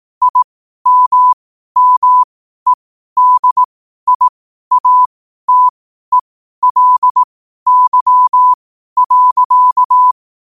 Morse code, heard as a keyed tone, about 18 wpm, spaced as for 12 wpm, 1000 Hz. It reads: IMMEDIATELY.
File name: qotdfiles / 12wpm